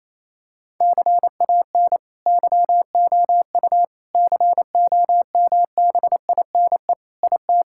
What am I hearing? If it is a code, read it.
Morse code, heard as a keyed tone, about 28 wpm, 700 Hz. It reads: CAN YOU COMBINE IT
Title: Sentences from Top 1600 Words 28wpm